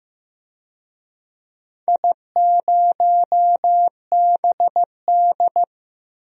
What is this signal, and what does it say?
Morse code, heard as a keyed tone, about 15 wpm, 700 Hz. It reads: I0BD
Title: DXCC-100 Call Signs - 4 Characters - Repeated 2x - 15wpm